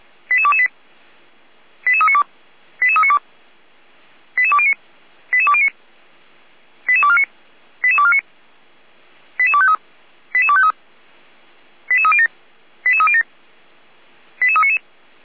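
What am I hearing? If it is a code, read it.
Five-tone selective call (ZVEI), heard as quick runs of five tones, tones of about 70 ms. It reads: 88228 88392 88392 88209 88209 88249 88249 88253 88253 88307 88307 88399
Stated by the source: ZVEI five-tone call received by SDR, from the feuerwehr-orlamuende.org fire brigade audio server.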